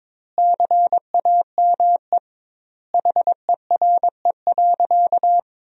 Morse code, heard as a keyed tone, about 22 wpm, 700 Hz. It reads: CAME HERE.
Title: 2x Word-Spacing - Sentences from Top 500 Words - 22wpm